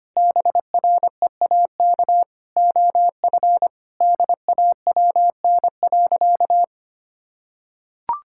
Morse code, heard as a keyed tone, about 25 wpm, 700 Hz. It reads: BREAK OF DAWN.
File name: practice / 25wpm / Difficult Sentences with Hints - 25wpm